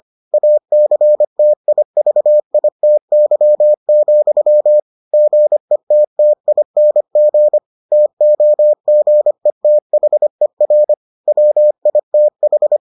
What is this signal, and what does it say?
Morse code, heard as a keyed tone, about 25 wpm, 600 Hz. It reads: ACTIVITY, GETTING TOGETHER WITH